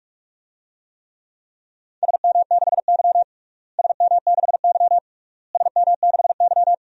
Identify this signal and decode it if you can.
Morse code, heard as a keyed tone, about 46 wpm, 700 Hz. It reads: SM6Y SM6Y SM6Y
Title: DXCC-100 Call Signs - 4 Characters - Repeated 3x - 45wpm